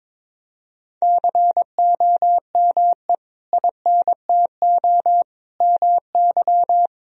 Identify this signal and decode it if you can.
Morse code, heard as a keyed tone, about 22 wpm, 700 Hz. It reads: COME INTO MY
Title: Sentences from Top 1000 Words - Review Part 1 - 22wpm